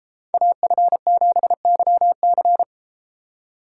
Morse code, heard as a keyed tone, about 32 wpm, 700 Hz. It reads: AF7YC